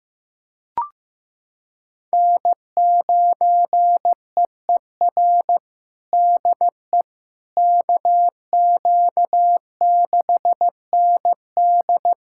Morse code, heard as a keyed tone, about 15 wpm, 700 Hz. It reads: N9EER DE KQ6ND
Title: USA Callsign DE Callsign - 5 Characters - 15wpm